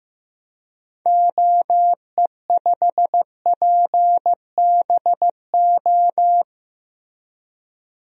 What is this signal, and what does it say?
Morse code, heard as a keyed tone, about 15 wpm, 700 Hz. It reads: OE5PBO
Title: DXCC-100 Call Signs - 6 Characters - Repeated 2x - 15wpm